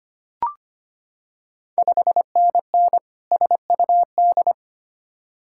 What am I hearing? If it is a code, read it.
Morse code, heard as a keyed tone, about 25 wpm, 700 Hz. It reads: 5NN SUD
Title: Ontario QSO Party Exchange 25wpm